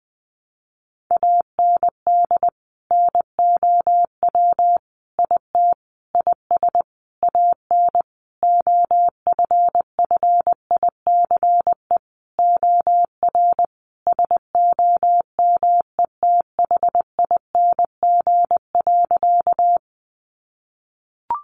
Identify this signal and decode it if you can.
Morse code, heard as a keyed tone, about 20 wpm, 700 Hz. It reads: AND NOW IT IS AN OFFICE OR SOMETHING.
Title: Sentences from Top 1000 Words 20wpm